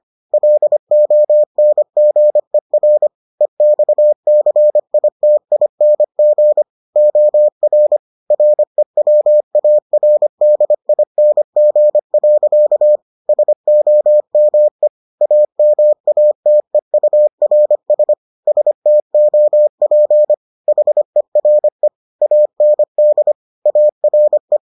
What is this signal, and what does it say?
Morse code, heard as a keyed tone, about 25 wpm, 600 Hz. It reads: LONGER EXCITING OR REWARDING. SOME AMATEURS STOP HERE AND ARE